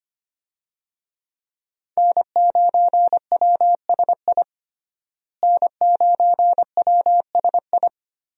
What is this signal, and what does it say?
Morse code, heard as a keyed tone, about 25 wpm, 700 Hz. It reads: N9WSI N9WSI
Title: USA Call Signs - 5 Characters - Repeated 3x - 25wpm